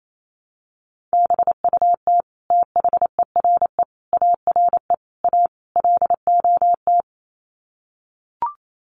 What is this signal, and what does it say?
Morse code, heard as a keyed tone, about 28 wpm, 700 Hz. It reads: BUT THERE ARE A LOT
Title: Sets of 5 Words Encore2 28wpm